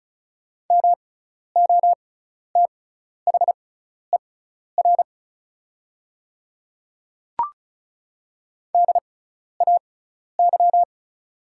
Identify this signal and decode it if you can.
Morse code, heard as a keyed tone, about 34 wpm, spaced as for 12 wpm, 690 Hz. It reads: MOTHER DAY
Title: 6x Character Spacing - Top 300 Words - 35wpm